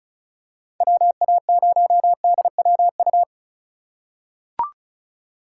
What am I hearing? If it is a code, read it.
Morse code, heard as a keyed tone, about 36 wpm, 700 Hz. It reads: WA0DWU
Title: USA Call Signs - 6 Characters - 4x Character Spacing - 35wpm